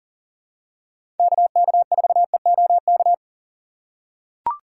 Morse code, heard as a keyed tone, about 40 wpm, 700 Hz. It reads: KK4EOK
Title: USA Call Signs - 6 Characters - 6x Character Spacing - 40wpm